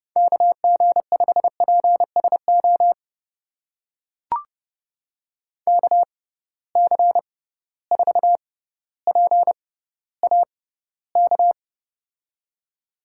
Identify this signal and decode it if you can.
Morse code, heard as a keyed tone, about 31 wpm, spaced as for 15 wpm, 700 Hz. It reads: KG5PSO KC4PAK
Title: USA Call Signs - 6 Characters - 6x Character Spacing - 30wpm